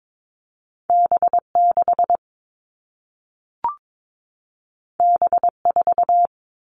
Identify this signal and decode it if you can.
Morse code, heard as a keyed tone, about 22 wpm, 700 Hz. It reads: B6 B4
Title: All Cut Numbers - 2-Digit 22wpm